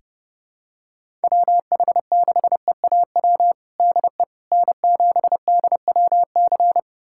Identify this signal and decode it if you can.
Morse code, heard as a keyed tone, about 32 wpm, 700 Hz. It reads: WH6EAW DE N7DWC
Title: USA Callsign DE Callsign 30wpm